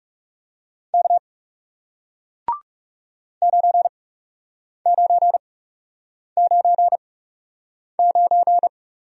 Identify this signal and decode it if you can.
Morse code, heard as a keyed tone, about 39 wpm, spaced as for 8 wpm, 700 Hz. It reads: K 9999